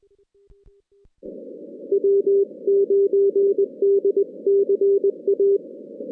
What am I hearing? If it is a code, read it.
Morse code, heard as a keyed tone, about 20 wpm, 400 Hz. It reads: W9DCA
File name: 2016 JAN CW NAQP / 2016 Jan naqp k5ot - 0-10